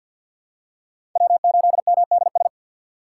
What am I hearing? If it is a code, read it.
Morse code, heard as a keyed tone, about 49 wpm, 700 Hz. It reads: W8MDS
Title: USA Call Signs - 5 Characters - 6x Character Spacing - 50wpm